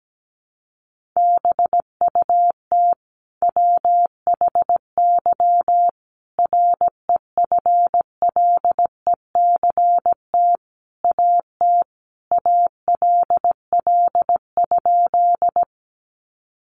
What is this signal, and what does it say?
Morse code, heard as a keyed tone, about 17 wpm, 700 Hz. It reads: BUT WHY REFLECT AT ALL?
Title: Sentences from Top 1500 Words 17wpm